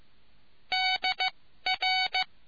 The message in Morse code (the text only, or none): DR